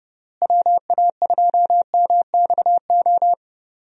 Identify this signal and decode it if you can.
Morse code, heard as a keyed tone, about 32 wpm, 700 Hz. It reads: WA2MXO
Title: USA Call Signs - 6 Characters - Repeated 2x - 30wpm